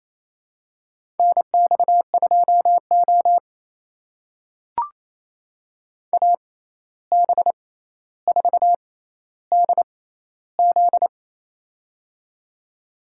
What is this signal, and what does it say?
Morse code, heard as a keyed tone, about 28 wpm, spaced as for 13 wpm, 700 Hz. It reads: NX2O AB4DZ
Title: USA Call Signs - 6x Character Spacing - 28wpm